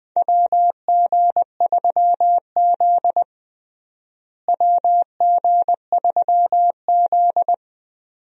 Morse code, heard as a keyed tone, about 20 wpm, 700 Hz. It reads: WG3Z WG3Z
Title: USA Call Signs - 4 Characters - Repeated 2x - 20wpm